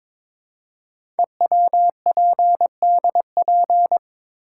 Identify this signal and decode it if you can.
Morse code, heard as a keyed tone, about 22 wpm, 700 Hz. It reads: EWPDP
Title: Code Groups - 5 Letters - 22wpm